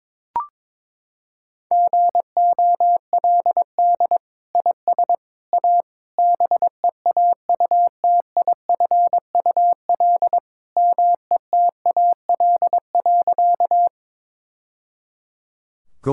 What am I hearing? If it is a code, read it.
Morse code, heard as a keyed tone, about 23 wpm, 700 Hz. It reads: GOLD IS A BEAUTIFUL METAL.
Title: Sentences from Top 1000 Words - Review Part 6 - 22wpm